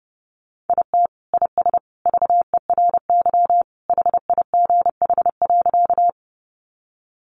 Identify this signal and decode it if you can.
Morse code, heard as a keyed tone, about 30 wpm, 700 Hz. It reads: IT IS VERY HIGH.
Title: Sentences from Top 200 Words 30wpm